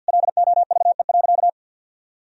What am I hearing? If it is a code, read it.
Morse code, heard as a keyed tone, about 50 wpm, 700 Hz. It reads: LOVE.
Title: Sentences from Top 1000 Words - Review Part 3 - 50wpm